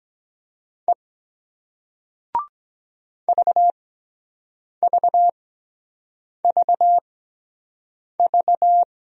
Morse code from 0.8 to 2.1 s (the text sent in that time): E